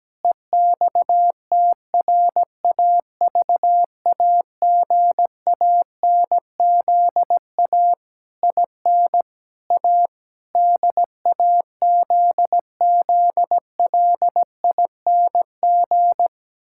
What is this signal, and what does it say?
Morse code, heard as a keyed tone, about 17 wpm, 700 Hz. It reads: EXTRAVAGANZA IN A DAZZLING
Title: Pangram Challenge v2 - 17wpm